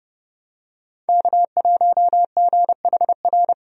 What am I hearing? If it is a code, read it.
Morse code, heard as a keyed tone, about 32 wpm, 700 Hz. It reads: K1GHR
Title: USA Call Signs - 5 Characters - 30wpm